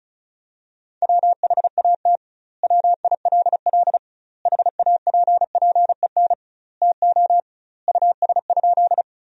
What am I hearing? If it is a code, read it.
Morse code, heard as a keyed tone, about 36 wpm, 700 Hz. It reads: WHAT WILL HAPPEN TO US?
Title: 2x Word-Spacing - Sentences from Top 400 Words - 35wpm